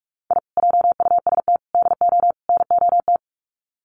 Morse code, heard as a keyed tone, about 44 wpm, 700 Hz. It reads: I JUST DO NOT
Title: Sets of 4 Words 45wpm